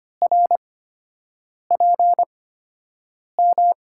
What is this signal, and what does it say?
Morse code, heard as a keyed tone, about 25 wpm, spaced as for 7 wpm, 700 Hz. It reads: RPM
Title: USA Call Signs - 5 Characters - 8x Character Spacing - 25wpm